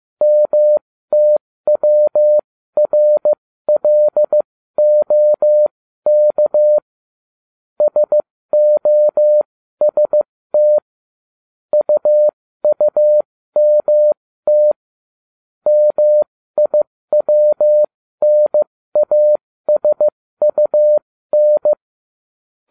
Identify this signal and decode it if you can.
Morse code, heard as a keyed tone, about 15 wpm, spaced as for 13 wpm, 600 Hz. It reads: MTWRLOK SOST UUMT MIWNASUN